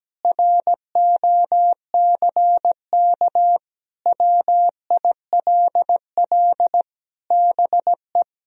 Morse code, heard as a keyed tone, about 17 wpm, 700 Hz. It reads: ROCK WILL BE
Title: Sentences from Top 1000 Words - Review Part 2 - 17wpm